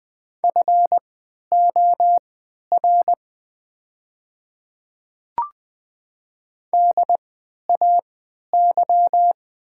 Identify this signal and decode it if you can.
Morse code, heard as a keyed tone, about 20 wpm, spaced as for 11 wpm, 690 Hz. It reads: FOR DAY
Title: 3x Character Spacing - Top 200 Words - 20wpm